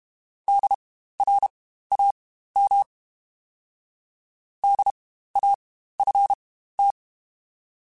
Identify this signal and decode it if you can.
Morse code, heard as a keyed tone, about 32 wpm, spaced as for 15 wpm, 800 Hz. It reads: DRAM DAFT